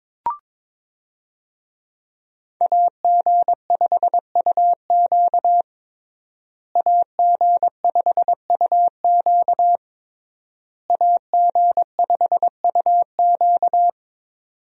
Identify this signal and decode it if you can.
Morse code, heard as a keyed tone, about 22 wpm, 700 Hz. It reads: AG5UQ AG5UQ AG5UQ
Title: USA Call Signs - 5 Characters - Repeated 3x - 22wpm